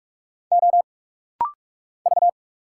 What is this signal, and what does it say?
Morse code, heard as a keyed tone, about 46 wpm, 690 Hz. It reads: O V
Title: Single Letters - Rapid-Fire 45wpm